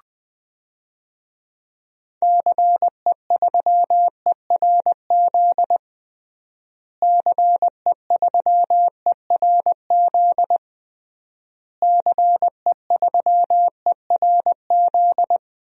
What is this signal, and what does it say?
Morse code, heard as a keyed tone, about 20 wpm, 700 Hz. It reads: CE3ERZ CE3ERZ CE3ERZ
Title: DXCC-100 Call Signs - 6 Characters - Repeated 3x - 20wpm